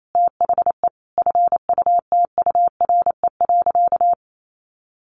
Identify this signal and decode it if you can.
Morse code, heard as a keyed tone, about 28 wpm, 700 Hz. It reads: THE FUTURE.
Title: Sentences from Top 1400 Words 28wpm